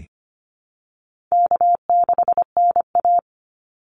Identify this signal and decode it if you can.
Morse code, heard as a keyed tone, about 25 wpm, 700 Hz. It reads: K6NA
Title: Easy Call Signs Repeated 3x - 25wpm